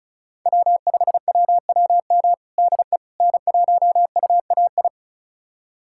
Morse code, heard as a keyed tone, about 36 wpm, 700 Hz. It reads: W5WWM DE N1UAI
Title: USA Callsign DE Callsign - 5 Characters - 35wpm